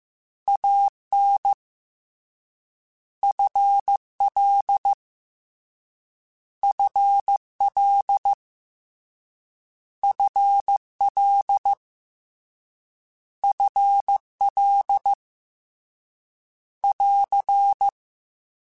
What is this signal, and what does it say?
Morse code, heard as a keyed tone, about 15 wpm, 800 Hz. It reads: A N F L F L F L F L <AR>
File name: Contrasts / Contrasts-mp3s / 2LtrContrasts @13